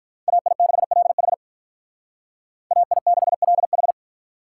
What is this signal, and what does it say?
Morse code, heard as a keyed tone, about 44 wpm, 700 Hz. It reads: AI6LH AI6LH